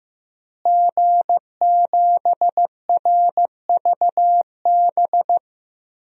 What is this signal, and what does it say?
Morse code, heard as a keyed tone, about 15 wpm, 700 Hz. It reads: G7RVB
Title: DXCC-100 Call Signs - 5 Characters - Repeated 2x - 15wpm